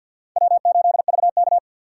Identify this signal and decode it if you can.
Morse code, heard as a keyed tone, about 62 wpm, 700 Hz. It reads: W8VK
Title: Speed Racing - USA 1x2 Call Signs 50wpm